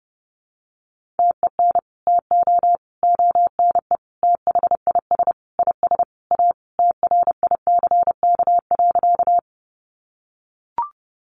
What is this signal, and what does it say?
Morse code, heard as a keyed tone, about 30 wpm, 700 Hz. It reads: TEN TO ONE THIS IS A TRICK.
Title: Sentences from Top 1500 Words 30wpm